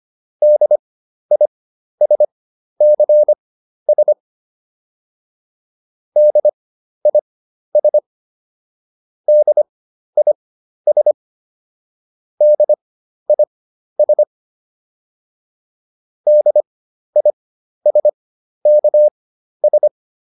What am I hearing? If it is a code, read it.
Morse code, heard as a keyed tone, about 25 wpm, spaced as for 12 wpm, 600 Hz. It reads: DISCS DIS DIS DIS DISKS